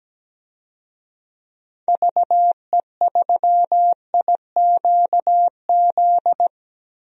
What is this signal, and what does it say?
Morse code, heard as a keyed tone, about 17 wpm, 700 Hz. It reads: VE3IQZ
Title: DXCC-100 Call Signs - 6 Characters - Repeated 3x - 17wpm